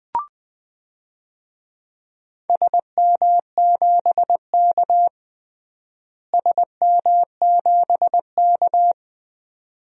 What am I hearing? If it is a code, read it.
Morse code, heard as a keyed tone, about 20 wpm, 700 Hz. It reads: SM7K SM7K
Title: DXCC-100 Call Signs - 4 Characters - Repeated 2x - 20wpm